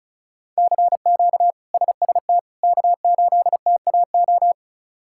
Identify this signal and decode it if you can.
Morse code, heard as a keyed tone, about 34 wpm, 700 Hz. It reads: CQ SST K8TAO
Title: K1USN SST QSO Practice - 35wpm